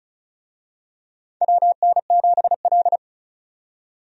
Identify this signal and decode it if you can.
Morse code, heard as a keyed tone, about 36 wpm, 700 Hz. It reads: WN7L
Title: USA Call Signs - 4 Characters - 35wpm